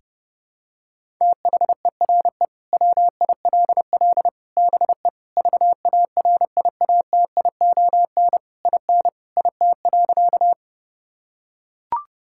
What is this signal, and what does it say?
Morse code, heard as a keyed tone, about 30 wpm, 700 Hz. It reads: THERE WILL BE VARIATION IN IT.
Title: Sentences from Top 1600 Words 30wpm